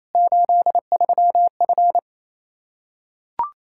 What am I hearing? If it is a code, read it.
Morse code, heard as a keyed tone, about 28 wpm, 700 Hz. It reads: 83F
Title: Practice QSO - Exchange 2 - Info - 28wpm